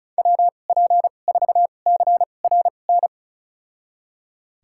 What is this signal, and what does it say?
Morse code, heard as a keyed tone, about 34 wpm, spaced as for 25 wpm, 700 Hz. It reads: WP4CRN
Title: USA Call Signs - 6 Characters - 2x Character Spacing - 35wpm